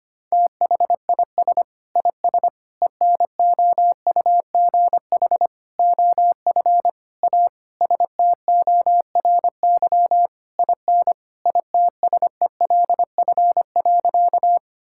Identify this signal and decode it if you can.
Morse code, heard as a keyed tone, about 25 wpm, 700 Hz. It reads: THIS IS ENOUGH OF A STORY IN ITSELF.
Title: Sentences from Top 1000 Words - Review Part 5 - 25wpm